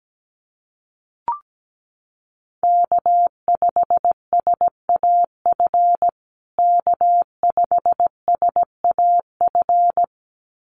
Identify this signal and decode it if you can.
Morse code, heard as a keyed tone, about 17 wpm, 700 Hz. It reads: K5SAF K5SAF
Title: USA Call Signs Repeated 2x 17wpm